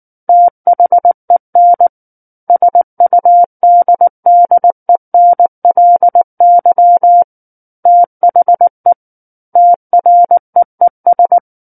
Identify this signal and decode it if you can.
Morse code, heard as a keyed tone, about 19 wpm, 700 Hz. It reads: THEN SUDDENLY THE TREES
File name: Book One / 10-Chapter 10 - 19 WPM